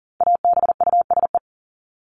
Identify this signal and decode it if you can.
Morse code, heard as a keyed tone, about 37 wpm, 700 Hz. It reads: ABUSE
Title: Top 1251-1500 Words 40wpm